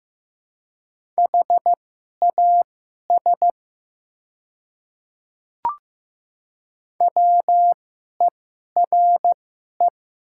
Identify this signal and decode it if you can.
Morse code, heard as a keyed tone, about 15 wpm, spaced as for 11 wpm, 700 Hz. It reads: HAS WERE
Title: 2x Character Spacing - Top 100 Words - 15wpm